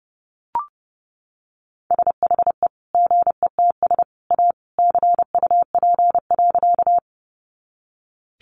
Morse code, heard as a keyed tone, about 30 wpm, 700 Hz. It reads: SHE GETS A CUP.
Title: Additional Easy Phrases 30wpm